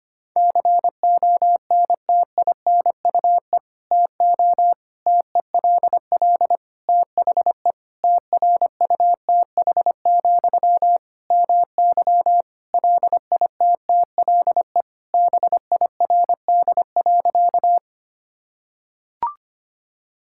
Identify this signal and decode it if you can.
Morse code, heard as a keyed tone, about 25 wpm, 700 Hz. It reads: CONTINUE TO TELL THE TRUTH, MY LITTLE BIRD.
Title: Sentences from Top 1000 Words - Review Part 6 - 25wpm